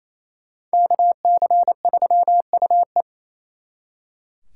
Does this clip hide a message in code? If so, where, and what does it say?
Yes: Morse code from 0.7 to 3.0 s, heard as a keyed tone, about 36 wpm, 700 Hz. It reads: KC3UE